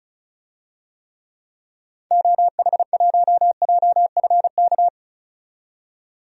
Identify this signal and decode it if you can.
Morse code, heard as a keyed tone, about 34 wpm, 700 Hz. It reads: OH1JFK
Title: DXCC-100 Call Signs - 6 Characters - 35wpm